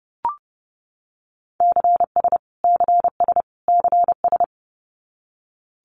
Morse code, heard as a keyed tone, about 29 wpm, 700 Hz. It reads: CS CS CS
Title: Abbreviations Repeated 3x - 30wpm